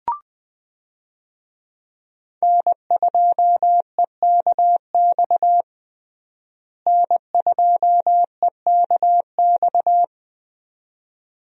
Morse code, heard as a keyed tone, about 20 wpm, 700 Hz. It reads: N2EKX N2EKX
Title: USA Call Signs - 5 Characters - Repeated 2x - 20wpm